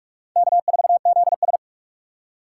Morse code, heard as a keyed tone, about 39 wpm, 700 Hz. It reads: K4ZS